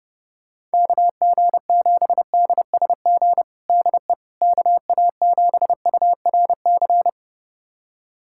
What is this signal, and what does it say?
Morse code, heard as a keyed tone, about 30 wpm, 700 Hz. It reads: KG7DSG DE KA7URC
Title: USA Callsign DE Callsign - 6 Characters - 30wpm